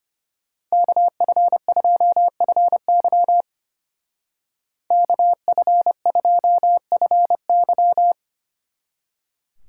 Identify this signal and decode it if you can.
Morse code, heard as a keyed tone, about 30 wpm, 700 Hz. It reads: KF2FY KF2FY
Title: Speed Racing - USA 2x2 Call Signs 25wpm